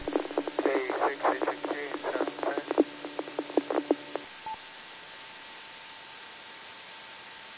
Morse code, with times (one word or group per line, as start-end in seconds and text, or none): none